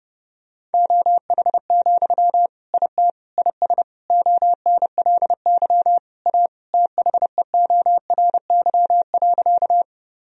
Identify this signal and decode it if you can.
Morse code, heard as a keyed tone, about 32 wpm, 700 Hz. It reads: OH, IT IS ONLY A THEORY.
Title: Sentences from Top 1000 Words - Review Part 3 - 30wpm